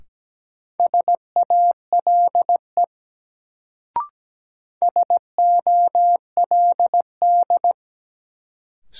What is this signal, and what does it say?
Morse code, heard as a keyed tone, about 17 wpm, 700 Hz. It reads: SALE SOLD
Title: Sentences from Top 1400 Words - New Word Drill - 17wpm